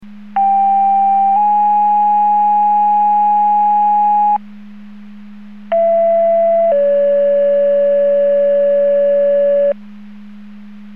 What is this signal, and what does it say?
Two-tone page, heard as a one-second tone, then a three-second tone, about 790 then 830 Hz, about 670 then 570 Hz. It is heard from 0.4 to 9.7 s.